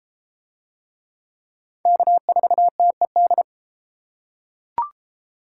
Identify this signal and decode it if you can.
Morse code, heard as a keyed tone, about 30 wpm, 700 Hz. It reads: K4TED